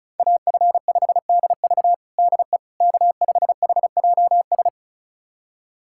Morse code, heard as a keyed tone, about 36 wpm, 700 Hz. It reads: AF5DV DE K5HJS
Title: USA Callsign DE Callsign - 5 Characters - 35wpm